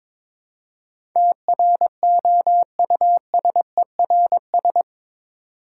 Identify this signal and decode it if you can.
Morse code, heard as a keyed tone, about 22 wpm, 700 Hz. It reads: TROUSERS
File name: practice / 22wpm / Eight Letter Words - 22wpm